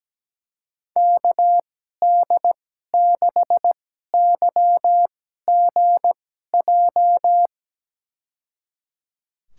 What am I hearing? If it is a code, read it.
Morse code, heard as a keyed tone, about 17 wpm, spaced as for 12 wpm, 700 Hz. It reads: KD6YGJ
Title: USA Call Signs - 6 Characters - 2x Character Spacing - 17wpm